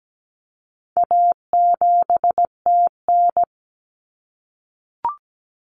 Morse code, heard as a keyed tone, about 17 wpm, 700 Hz. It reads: A7TN
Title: Common Cut Numbers - 4-Digit 17wpm